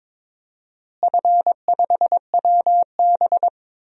Morse code, heard as a keyed tone, about 22 wpm, 700 Hz. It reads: F5WB
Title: DXCC-100 Call Signs - 4 Characters - Repeated 2x - 22wpm